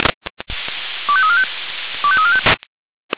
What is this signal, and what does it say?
Five-tone selective call (ZVEI), heard as quick runs of five tones, tones of about 70 ms. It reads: 25345 25345